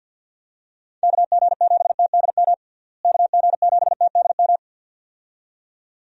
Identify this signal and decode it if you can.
Morse code, heard as a keyed tone, about 47 wpm, 700 Hz. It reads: KG7TDM KG7TDM
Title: USA Call Signs - 6 Characters - Repeated 2x - 50wpm